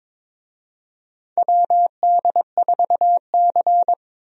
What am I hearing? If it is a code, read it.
Morse code, heard as a keyed tone, about 22 wpm, 700 Hz. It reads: WD4C
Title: USA Call Signs - 4 Characters - 22wpm